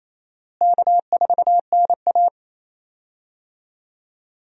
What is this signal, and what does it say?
Morse code, heard as a keyed tone, about 28 wpm, 700 Hz. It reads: K4NA
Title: USA Call Signs - 4 Characters - Repeated 2x - 28wpm